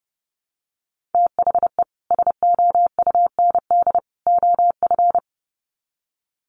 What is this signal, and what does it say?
Morse code, heard as a keyed tone, about 30 wpm, 700 Hz. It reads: THE SOUND OF
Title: Sets of 3 Words 30wpm